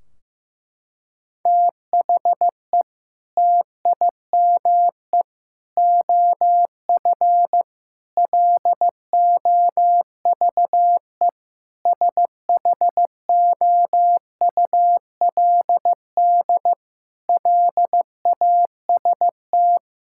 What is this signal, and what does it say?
Morse code, heard as a keyed tone, about 15 wpm, 700 Hz. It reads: THE TIME OF LOVE SHOULD LAST
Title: Sentences from Top 1600 Words 15wpm